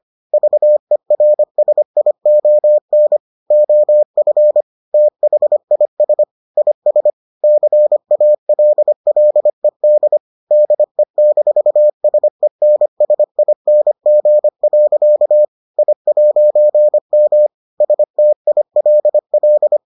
Morse code, heard as a keyed tone, about 25 wpm, 600 Hz. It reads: VERSION OF THIS IS CALLED DE-SENSING. I'M STILL